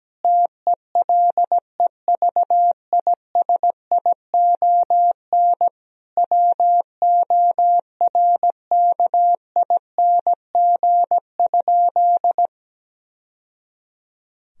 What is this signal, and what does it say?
Morse code, heard as a keyed tone, about 17 wpm, 700 Hz. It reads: TELEVISION WORKING?
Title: Sentences from Top 1000 Words - Review Part 4 - 17wpm